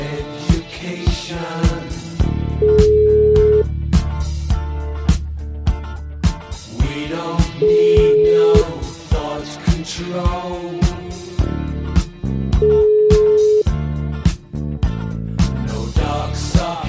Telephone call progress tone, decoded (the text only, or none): ringback tone